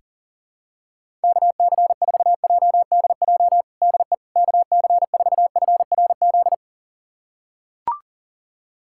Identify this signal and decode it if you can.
Morse code, heard as a keyed tone, about 40 wpm, 700 Hz. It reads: KC4JDJ DE KC4FRZ
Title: USA Callsign DE Callsign - 6 Characters - 40wpm